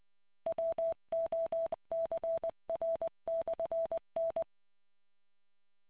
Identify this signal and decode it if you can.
Morse code, heard as a keyed tone, about 20 wpm, 660 Hz. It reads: WOCR/N